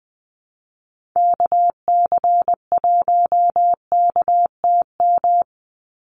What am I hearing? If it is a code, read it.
Morse code, heard as a keyed tone, about 20 wpm, 700 Hz. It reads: KC1KTM
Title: USA Call Signs 20wpm